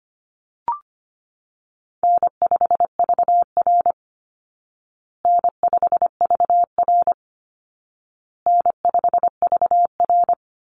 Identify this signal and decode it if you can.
Morse code, heard as a keyed tone, about 25 wpm, 700 Hz. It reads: N5VR N5VR N5VR
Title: Easy Call Signs Repeated 3x - 25wpm